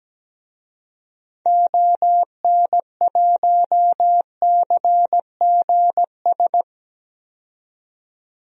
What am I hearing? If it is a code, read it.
Morse code, heard as a keyed tone, about 17 wpm, 700 Hz. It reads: ON1CGS
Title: DXCC-100 Call Signs - 6 Characters - 17wpm